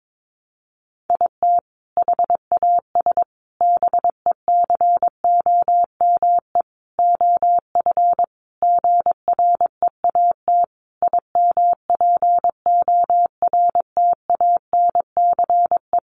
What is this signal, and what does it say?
Morse code, heard as a keyed tone, about 22 wpm, 700 Hz. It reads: IT HAS BECOME OF GREAT IMPORTANCE
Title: Sentences from Top 1200 Words 22wpm